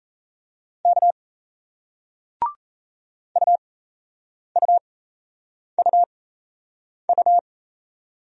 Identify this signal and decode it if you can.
Morse code, heard as a keyed tone, about 36 wpm, spaced as for 8 wpm, 700 Hz. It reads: K UUUU